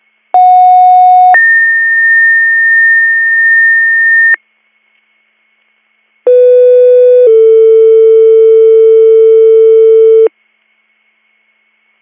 Two-tone page, heard as a one-second tone, then a three-second tone, about 730 then 1800 Hz, about 500 then 450 Hz. It is heard from 0.3 to 10.3 s.